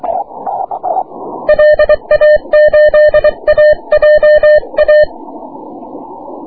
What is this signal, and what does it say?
Morse code, heard as a keyed tone, about 23 wpm, 600 Hz. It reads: LA8AJA
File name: wp / 6m aurora cw 22jul 2009